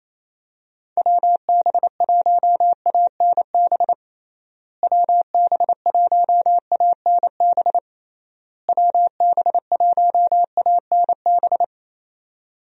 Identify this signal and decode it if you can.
Morse code, heard as a keyed tone, about 27 wpm, 700 Hz. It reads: WB1ANB WB1ANB WB1ANB
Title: USA Call Signs - 6 Characters - Repeated 3x - 28wpm